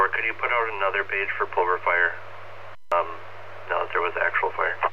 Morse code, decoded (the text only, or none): none